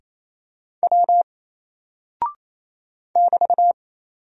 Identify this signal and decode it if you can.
Morse code, heard as a keyed tone, about 27 wpm, 700 Hz. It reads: W <BT>
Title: Single Character (Letter:Number:Punctuation:Prosign) 28wpm